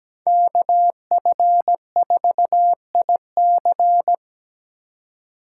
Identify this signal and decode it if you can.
Morse code, heard as a keyed tone, about 20 wpm, 700 Hz. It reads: KF4IC
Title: Speed Racing - USA 2x2 Call Signs 17wpm